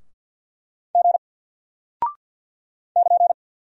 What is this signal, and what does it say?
Morse code, heard as a keyed tone, about 49 wpm, 700 Hz. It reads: G <KN>
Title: Single Character (Letter:Number:Punctuation:Prosign) Alt 50wpm